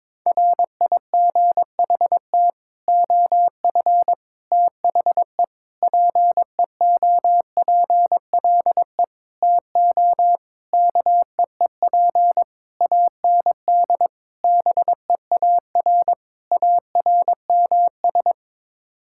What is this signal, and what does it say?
Morse code, heard as a keyed tone, about 22 wpm, 700 Hz. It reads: RIGHT OF THE PEOPLE TO KEEP AND BEAR ARMS
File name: Sets of 9 Words - 22wpm